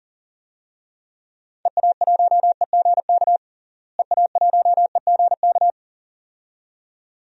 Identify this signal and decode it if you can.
Morse code, heard as a keyed tone, about 40 wpm, 700 Hz. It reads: EA1EGK EA1EGK
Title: DXCC-100 Call Signs - 6 Characters - Repeated 2x - 40wpm